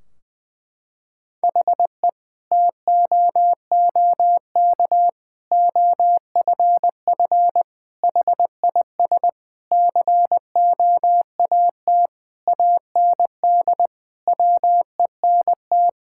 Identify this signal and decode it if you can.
Morse code, heard as a keyed tone, about 21 wpm, 700 Hz. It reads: HE TOOK OFF HIS COAT AND WENT
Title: Sentences from Top 1600 Words 20wpm